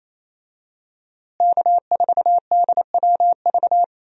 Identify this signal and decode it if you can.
Morse code, heard as a keyed tone, about 28 wpm, 700 Hz. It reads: K4DWV
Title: USA Call Signs - 5 Characters - Repeated 2x - 28wpm